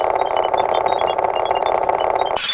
There is no code.